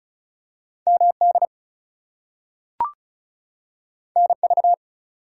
Morse code, heard as a keyed tone, about 36 wpm, 700 Hz. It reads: MD NV
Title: US States and Canadian Provinces 35wpm